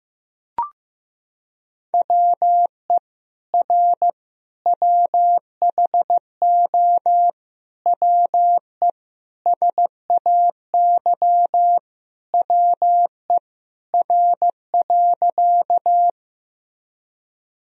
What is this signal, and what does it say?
Morse code, heard as a keyed tone, about 15 wpm, 700 Hz. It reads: WE R WHO WE SAY WE R.
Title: Sentences from Top 200 Words and Abbreviations 15wpm